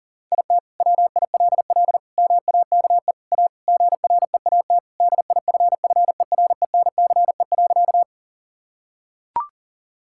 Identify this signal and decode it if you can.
Morse code, heard as a keyed tone, about 40 wpm, 700 Hz. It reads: IT WILL MAKE A GREAT DIFFERENCE.
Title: Sentences from Top 1000 Words - Review Part 5 - 40wpm